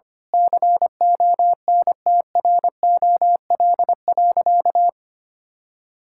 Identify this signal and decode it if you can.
Morse code, heard as a keyed tone, about 25 wpm, 700 Hz. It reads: CONTROL.
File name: practice / 25wpm / Sentences with 9-Word Phrases - 25wpm